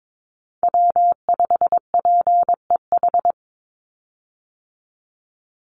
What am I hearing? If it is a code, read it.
Morse code, heard as a keyed tone, about 22 wpm, 700 Hz. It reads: W5PEH
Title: Call Signs Repeated 2x - 22wpm